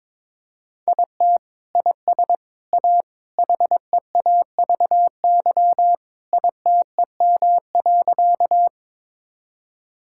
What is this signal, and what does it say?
Morse code, heard as a keyed tone, about 22 wpm, 700 Hz. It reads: IT IS A HEAVY ITEM.
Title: Sentences from Top 1000 Words - Review Part 6 - 22wpm